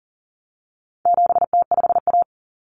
Morse code, heard as a keyed tone, about 37 wpm, 700 Hz. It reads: 7T5A
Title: Common Cut Numbers - 4-Digit 40wpm